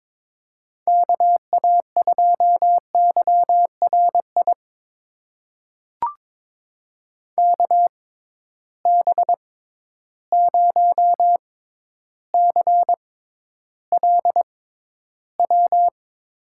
Morse code, heard as a keyed tone, about 22 wpm, spaced as for 11 wpm, 700 Hz. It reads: KA2YRI KB0CLW
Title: USA Call Signs - 6 Characters - 6x Character Spacing - 22wpm